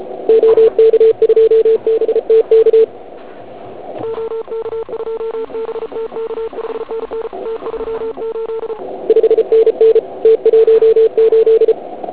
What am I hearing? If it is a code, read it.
Morse code, heard as a keyed tone, about 34 wpm, 440 Hz. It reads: OK2BTK OK2BTK 5NN T38 5NN T18
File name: Archiv / CUC106 / cuc106 PILEUP